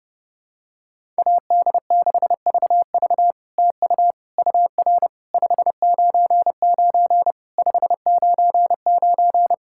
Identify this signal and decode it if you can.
Morse code, heard as a keyed tone, about 30 wpm, 700 Hz. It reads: AD6VV TU UR 599 599
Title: SOTA QSO Practice 30wpm